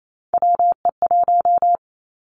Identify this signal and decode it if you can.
Morse code, heard as a keyed tone, about 28 wpm, 700 Hz. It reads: WE1
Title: Three Challenging Letter-Number Combinations 28wpm